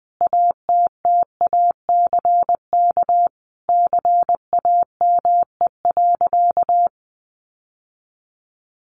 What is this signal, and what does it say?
Morse code, heard as a keyed tone, about 20 wpm, 700 Hz. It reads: ATTACK CAME.
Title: Sentences from Top 1500 Words 20wpm